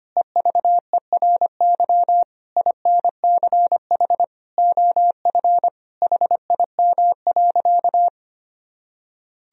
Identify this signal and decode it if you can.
Morse code, heard as a keyed tone, about 25 wpm, 700 Hz. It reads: EVERY INCH OF HIM.
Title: Sentences from Top 1000 Words - Review Part 3 - 25wpm